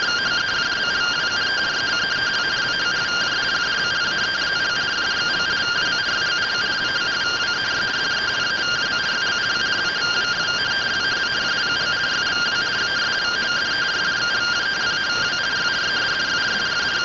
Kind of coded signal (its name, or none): two-tone teleprinter signal